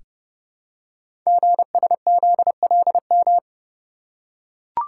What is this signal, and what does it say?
Morse code, heard as a keyed tone, about 29 wpm, 700 Hz. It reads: GSZLM
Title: Code Groups - 5 Letters - 30wpm